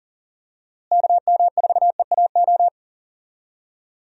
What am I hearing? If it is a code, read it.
Morse code, heard as a keyed tone, about 40 wpm, 700 Hz. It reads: KM4EAO
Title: USA Call Signs - 6 Characters - Repeated 3x - 40wpm